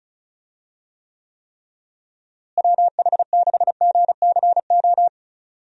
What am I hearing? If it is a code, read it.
Morse code, heard as a keyed tone, about 36 wpm, 700 Hz. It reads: WH6GCO
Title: USA Call Signs - 6 Characters - Repeated 3x - 35wpm